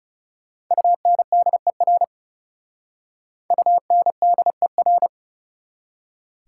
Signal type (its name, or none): Morse code